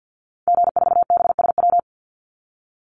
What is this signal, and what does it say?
Morse code, heard as a keyed tone, about 50 wpm, 700 Hz. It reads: G4BSW
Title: Call Signs Repeated 2x - 50wpm